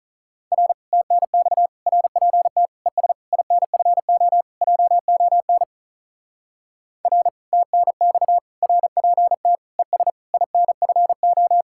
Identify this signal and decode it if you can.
Morse code, heard as a keyed tone, about 42 wpm, 700 Hz. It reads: R TNX RPT ES INFO JON R TNX RPT ES INFO
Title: Speed-Racing Common QSO Phrases - 35wpm